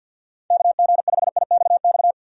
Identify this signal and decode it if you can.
Morse code, heard as a keyed tone, about 51 wpm, 700 Hz. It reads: KG5IXX
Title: USA Call Signs - 6 Characters - 50wpm